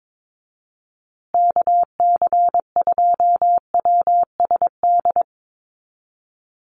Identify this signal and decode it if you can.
Morse code, heard as a keyed tone, about 22 wpm, 700 Hz. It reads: KC2WSD
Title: USA Call Signs Repeated 3x 22wpm